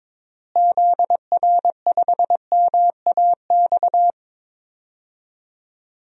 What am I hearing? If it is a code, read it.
Morse code, heard as a keyed tone, about 22 wpm, 700 Hz. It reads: ZR5MAX
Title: DXCC-100 Call Signs - 6 Characters - Repeated 3x - 22wpm